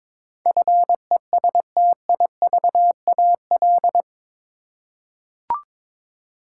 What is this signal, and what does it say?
Morse code, heard as a keyed tone, about 21 wpm, 700 Hz. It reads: FESTIVAL
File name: practice / 22wpm / Eight Letter Words - 22wpm